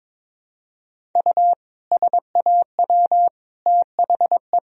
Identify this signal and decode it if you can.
Morse code, heard as a keyed tone, about 22 wpm, 700 Hz. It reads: U SAW THE
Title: Sentences from Top 400 Words and Abbreviations 22wpm